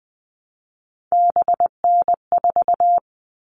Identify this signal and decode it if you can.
Morse code, heard as a keyed tone, about 20 wpm, 700 Hz. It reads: BN4
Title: All Cut Numbers - 3-Digit 20wpm